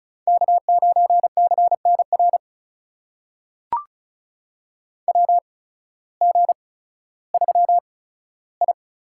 Morse code, heard as a keyed tone, about 36 wpm, spaced as for 16 wpm, 700 Hz. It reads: K9CNR WG3I
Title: USA Call Signs - 8x Character Spacing - 35wpm